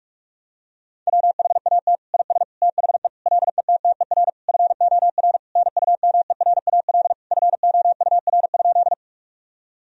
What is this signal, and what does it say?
Morse code, heard as a keyed tone, about 42 wpm, 700 Hz. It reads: WHAT IS THE LETTER FOR NUMERAL FOUR?
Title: Sentences from Top 1000 Words - Review Part 2 - 45wpm